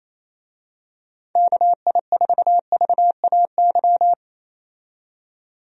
Morse code, heard as a keyed tone, about 28 wpm, 700 Hz. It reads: KI4VAY
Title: USA Call Signs - 6 Characters - 28wpm